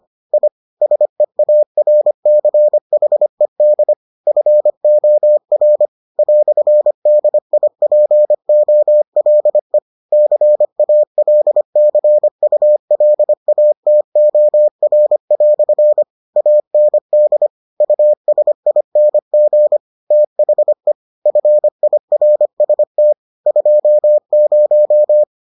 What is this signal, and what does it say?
Morse code, heard as a keyed tone, about 25 wpm, 600 Hz. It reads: I SEARCHED FOR "DIPOLE CALCULATOR" AND USING THE FIRST 20